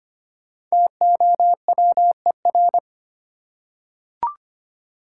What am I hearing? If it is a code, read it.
Morse code, heard as a keyed tone, about 25 wpm, 700 Hz. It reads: TOWER
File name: 5 Letter Words - First 3 Letters Given - 25wpm